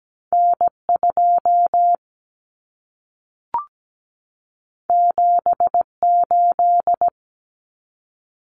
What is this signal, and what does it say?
Morse code, heard as a keyed tone, about 17 wpm, 700 Hz. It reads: N2 78
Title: Common Cut Numbers - 2-Digit 17wpm